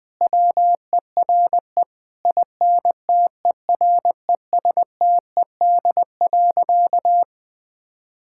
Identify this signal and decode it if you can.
Morse code, heard as a keyed tone, about 20 wpm, 700 Hz. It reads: WERE INTERESTED.
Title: Sentences from Top 1000 Words - Review Part 6 - 20wpm